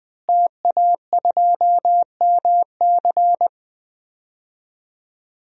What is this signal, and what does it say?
Morse code, heard as a keyed tone, about 20 wpm, 700 Hz. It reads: TA2MC
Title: DXCC-100 Call Signs - 5 Characters - Repeated 3x - 20wpm